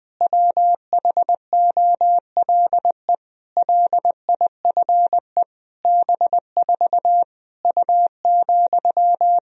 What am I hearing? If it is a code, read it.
Morse code, heard as a keyed tone, about 20 wpm, 700 Hz. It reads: WHOLE LIFE B4 U,
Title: Sentences from Top 400 Words and Abbreviations 20wpm